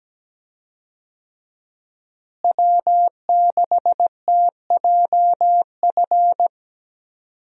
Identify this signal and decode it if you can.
Morse code, heard as a keyed tone, about 17 wpm, 700 Hz. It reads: W6TJF
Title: USA Call Signs - 5 Characters - Repeated 3x - 17wpm